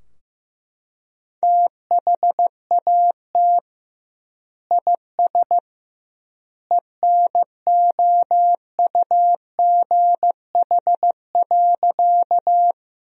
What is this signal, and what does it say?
Morse code, heard as a keyed tone, about 15 wpm, 700 Hz. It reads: THAT IS ENOUGH.
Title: 2x Word-Spacing - Sentences from Top 400 Words - 15wpm